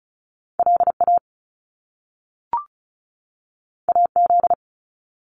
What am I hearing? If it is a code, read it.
Morse code, heard as a keyed tone, about 36 wpm, 700 Hz. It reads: LA AZ
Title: US States 35wpm